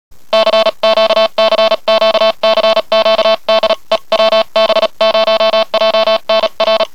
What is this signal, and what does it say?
Morse code, heard as a keyed tone, about 35 wpm, 660 Hz, with noise. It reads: CQCQCQ DE WB0JNR